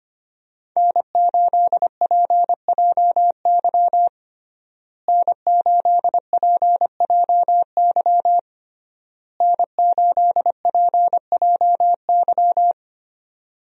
Morse code, heard as a keyed tone, about 25 wpm, 700 Hz. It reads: N8PJY N8PJY N8PJY
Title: USA Call Signs - 5 Characters - Repeated 3x - 25wpm